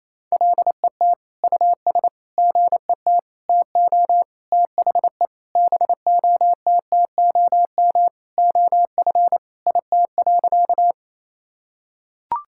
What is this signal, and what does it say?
Morse code, heard as a keyed tone, about 28 wpm, 700 Hz. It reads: LET US GET TO THE BOTTOM OF IT.
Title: Sentences from Top 1000 Words - Review Part 6 - 28wpm